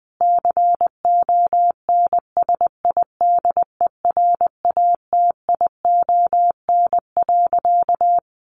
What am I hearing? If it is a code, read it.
Morse code, heard as a keyed tone, about 20 wpm, 700 Hz. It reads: CONSIDERATION.
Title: Sentences from Top 1500 Words 20wpm